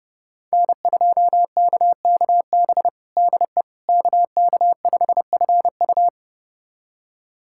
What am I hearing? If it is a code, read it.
Morse code, heard as a keyed tone, about 30 wpm, 700 Hz. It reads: N2KKB DE KK5FU
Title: USA Callsign DE Callsign 30wpm